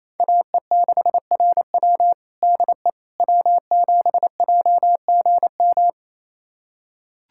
Morse code, heard as a keyed tone, about 28 wpm, 700 Hz. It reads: AE6RW DE W7JGM
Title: USA Callsign DE Callsign - 5 Characters - 28wpm